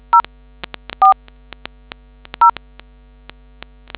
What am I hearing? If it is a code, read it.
DTMF, heard as telephone touch tones, tones of about 110 ms. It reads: 0 4 0